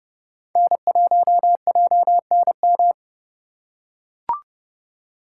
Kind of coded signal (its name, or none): Morse code